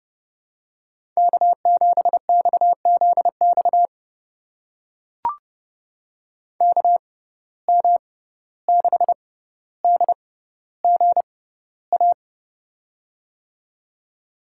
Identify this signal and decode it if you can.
Morse code, heard as a keyed tone, about 30 wpm, spaced as for 15 wpm, 700 Hz. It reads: K7XZX KM6DGA
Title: USA Call Signs - 6x Character Spacing - 30wpm